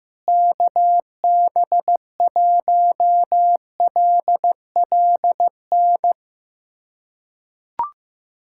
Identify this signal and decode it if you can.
Morse code, heard as a keyed tone, about 15 wpm, 700 Hz. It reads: KB1LLN
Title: USA Callsign DE Callsign - 6 Characters - 15wpm